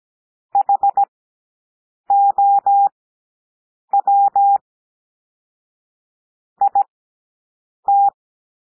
Morse code, heard as a keyed tone, about 17 wpm, spaced as for 7 wpm, 800 Hz. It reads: HOW IT
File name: SOUND PLAIN 17 wpm